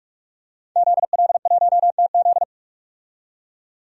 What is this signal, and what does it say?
Morse code, heard as a keyed tone, about 47 wpm, 700 Hz. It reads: ZL1TZ
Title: DXCC-100 Call Signs - 5 Characters - 45wpm